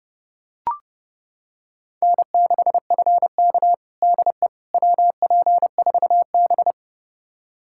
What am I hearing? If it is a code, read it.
Morse code, heard as a keyed tone, about 30 wpm, 700 Hz. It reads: N6FK DE WP4B
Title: USA Callsign DE Callsign - 4 Characters - 30wpm